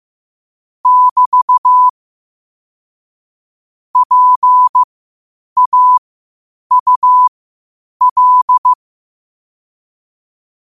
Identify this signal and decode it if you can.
Morse code, heard as a keyed tone, about 15 wpm, spaced as for 8 wpm, 1000 Hz. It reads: <BT> PAUL
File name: qotdfiles / 8wpm